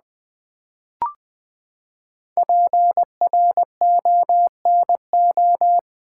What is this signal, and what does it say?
Morse code, heard as a keyed tone, about 20 wpm, 700 Hz. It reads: PRONO
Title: Build A Word - 5 Letters At A Time - 10-Letter Words - 20wpm